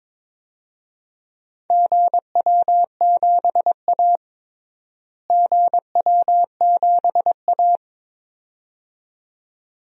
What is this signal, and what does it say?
Morse code, heard as a keyed tone, about 22 wpm, 700 Hz. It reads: GW7A GW7A
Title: DXCC-100 Call Signs - 4 Characters - Repeated 2x - 22wpm